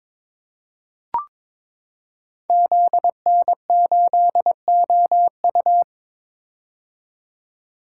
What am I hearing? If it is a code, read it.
Morse code, heard as a keyed tone, about 22 wpm, 700 Hz. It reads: ZN8OU
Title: Code Groups - 5 Letter:Numbers - 22wpm